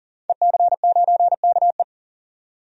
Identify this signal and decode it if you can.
Morse code, heard as a keyed tone, about 40 wpm, 700 Hz. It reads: EC9KE
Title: DXCC-100 Call Signs - 5 Characters - 40wpm